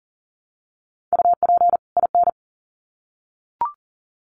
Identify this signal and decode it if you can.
Morse code, heard as a keyed tone, about 40 wpm, 700 Hz. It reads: UP IN
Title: Sets of 2 Words 40wpm